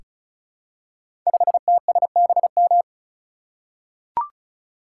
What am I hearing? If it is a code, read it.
Morse code, heard as a keyed tone, about 36 wpm, 700 Hz. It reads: 5TSBM
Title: Code Groups - 5 Letter:Numbers - 35wpm